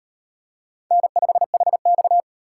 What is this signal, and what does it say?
Morse code, heard as a keyed tone, about 37 wpm, 700 Hz. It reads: N5HX